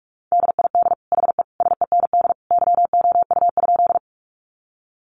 Morse code, heard as a keyed tone, about 44 wpm, 700 Hz. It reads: DID HE SEND YOU?
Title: Sentences from Top 1500 Words 45wpm